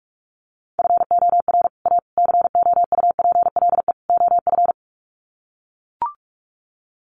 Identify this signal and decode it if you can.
Morse code, heard as a keyed tone, about 44 wpm, 700 Hz. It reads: FOR A COUPLE OF
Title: Sets of 4 Words 45wpm